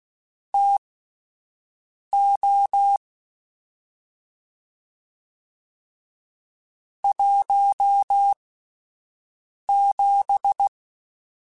Morse code, heard as a keyed tone, about 16 wpm, spaced as for 5 wpm, 800 Hz. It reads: TO 17
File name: Spaced Characters / Spaced Characters-mp3s / Weather4 21 sp @10